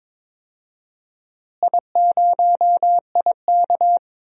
Morse code, heard as a keyed tone, about 22 wpm, 700 Hz. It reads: I0IK
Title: DXCC-100 Call Signs - 4 Characters - Repeated 2x - 22wpm